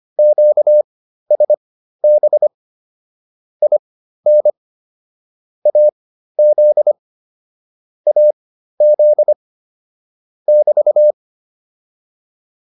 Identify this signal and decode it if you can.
Morse code, heard as a keyed tone, about 25 wpm, spaced as for 13 wpm, 600 Hz. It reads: QSB IN AZ AZ <BT>
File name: QSO204 13